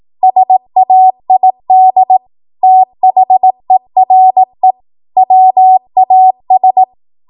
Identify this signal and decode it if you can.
Morse code, heard as a keyed tone, about 18 wpm, 750 Hz. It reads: SAID THERE WAS
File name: Archive / 18 WPM / 18WPM